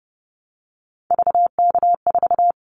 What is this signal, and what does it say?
Morse code, heard as a keyed tone, about 30 wpm, 700 Hz. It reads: VK4
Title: Callsign Prefixes for Top 100 DXCC - 30wpm